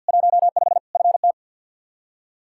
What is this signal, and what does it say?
Morse code, heard as a keyed tone, about 53 wpm, 700 Hz. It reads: 15 FT
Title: Practice QSO - Exchange 2 - Info - 50wpm